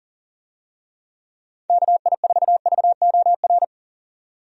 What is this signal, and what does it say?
Morse code, heard as a keyed tone, about 40 wpm, 700 Hz. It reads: KI4VOR
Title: USA Call Signs - 6 Characters - 40wpm